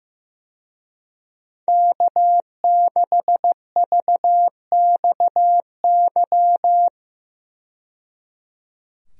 Morse code, heard as a keyed tone, about 15 wpm, 700 Hz. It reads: K6VXY
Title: USA Call Signs - 5 Characters - 15wpm